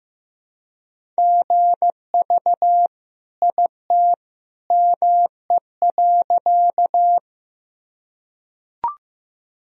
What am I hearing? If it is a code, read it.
Morse code, heard as a keyed tone, about 15 wpm, 700 Hz. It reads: GV IT ME.
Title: Sentences from Top 200 Words and Abbreviations 15wpm